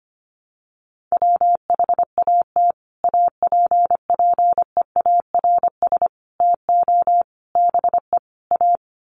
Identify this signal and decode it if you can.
Morse code, heard as a keyed tone, about 25 wpm, 700 Hz. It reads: WHAT APPEARS TO BE A
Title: Sets of 5 Words Encore2 25wpm